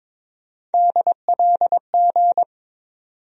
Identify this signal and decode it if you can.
Morse code, heard as a keyed tone, about 22 wpm, 700 Hz. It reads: DLG
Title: USA Call Sign Postfixes - 3 Letters 22wpm